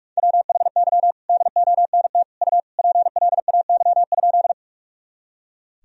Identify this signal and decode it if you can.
Morse code, heard as a keyed tone, about 46 wpm, 700 Hz. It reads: WHY DONT U PLAY?
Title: Sentences from Top 300 Words and Abbreviations 45wpm